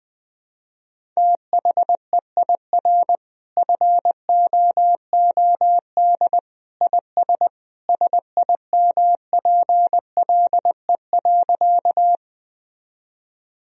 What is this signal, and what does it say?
Morse code, heard as a keyed tone, about 20 wpm, 700 Hz. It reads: THEIR FOOD IS SIMPLE.
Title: Sentences from Top 1000 Words - Review Part 1 - 20wpm